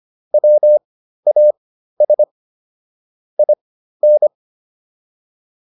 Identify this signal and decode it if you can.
Morse code, heard as a keyed tone, about 25 wpm, spaced as for 13 wpm, 600 Hz. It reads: WAS IN